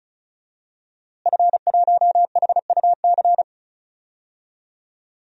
Morse code, heard as a keyed tone, about 36 wpm, 700 Hz. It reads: F1HUC
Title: DXCC-100 Call Signs - 5 Characters - Repeated 3x - 35wpm